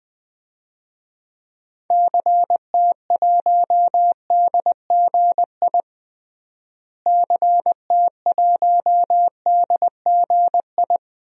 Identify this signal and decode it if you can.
Morse code, heard as a keyed tone, about 20 wpm, 700 Hz. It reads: CT1DGI CT1DGI
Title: DXCC-100 Call Signs - 6 Characters - Repeated 2x - 20wpm